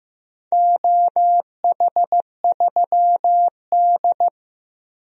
Morse code, heard as a keyed tone, about 15 wpm, 700 Hz. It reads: OH3D
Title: DXCC-100 Call Signs - 4 Characters - Repeated 2x - 15wpm